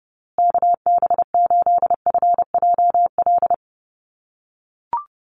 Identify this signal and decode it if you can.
Morse code, heard as a keyed tone, about 30 wpm, 700 Hz. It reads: KB8FJL
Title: USA Call Signs Repeated 3x 30wpm